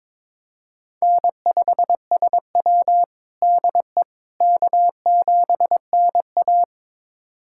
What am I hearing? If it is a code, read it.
Morse code, heard as a keyed tone, about 22 wpm, 700 Hz. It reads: N5SW DE K7NA
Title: USA Callsign DE Callsign - 4 Characters - 22wpm